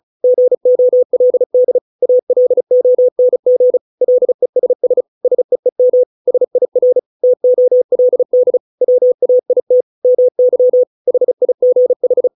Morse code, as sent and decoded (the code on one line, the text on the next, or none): --. --- .-.. -.. / .- .-.. --- -. --. / .-.. ... ... / ... . -- / ... .. .-. / - --- .-.. -.. / .-- .- .. - / -- -.-- / .... .. --. ....
GOLD ALONG LESS SEEM SIR TOLD WAIT MY HIGH